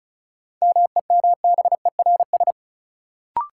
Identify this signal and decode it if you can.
Morse code, heard as a keyed tone, about 36 wpm, 700 Hz. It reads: MEMBERS
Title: Sets of 3 Words - New Word Drill - 35wpm